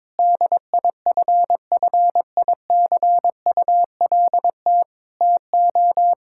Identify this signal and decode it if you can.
Morse code, heard as a keyed tone, about 22 wpm, 700 Hz. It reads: DIFFICULT TO
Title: Sentences from Top 1600 Words 22wpm